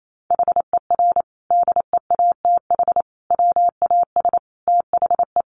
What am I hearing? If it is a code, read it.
Morse code, heard as a keyed tone, about 28 wpm, 700 Hz. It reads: HER DEATH WAS THE
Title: Sentences from Top 1300 Words 28wpm